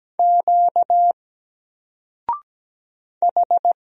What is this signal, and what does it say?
Morse code, heard as a keyed tone, about 17 wpm, 700 Hz. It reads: Q H
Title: Single Letter-Number Alt 17wpm